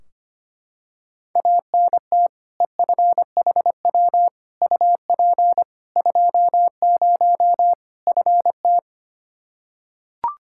Morse code, heard as a keyed tone, about 25 wpm, 700 Hz. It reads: ANT EFHW UP 20 FT
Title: Practice QSO - Exchange 2 - Info - 25wpm